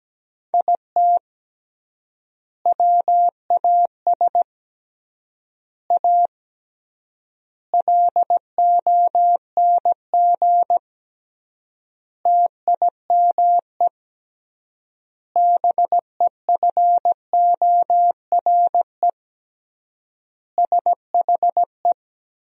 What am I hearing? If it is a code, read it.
Morse code, heard as a keyed tone, about 17 wpm, 700 Hz. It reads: IT WAS A LONG TIME BEFORE SHE
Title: 3x Word-Spacing - Sentences from Top 500 Words - 17wpm